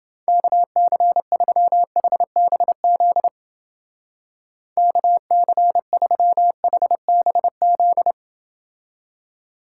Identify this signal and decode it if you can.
Morse code, heard as a keyed tone, about 30 wpm, 700 Hz. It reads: KC3HBZ KC3HBZ